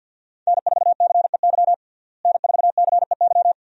Morse code, heard as a keyed tone, about 53 wpm, 700 Hz. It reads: N4CEY N4CEY
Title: USA Call Signs - 5 Characters - Repeated 2x - 50wpm